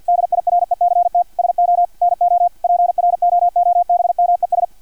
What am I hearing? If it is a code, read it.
Morse code, heard as a keyed tone, about 50 wpm, 700 Hz, with radio noise. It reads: DIRECT SO NO PROOBMES
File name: archived / CRSnet